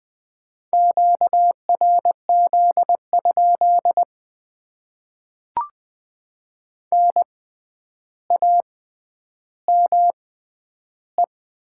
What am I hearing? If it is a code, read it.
Morse code, heard as a keyed tone, about 20 wpm, spaced as for 10 wpm, 700 Hz. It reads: QRZ? NAME
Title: Top 90 QSO Elements - 6x Character Spacing - 20wpm